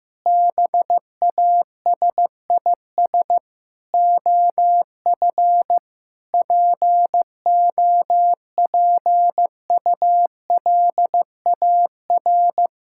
Morse code, heard as a keyed tone, about 15 wpm, 700 Hz. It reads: BASIS OF POPULAR
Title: Sentences from Top 1000 Words - Review Part 5 - 15wpm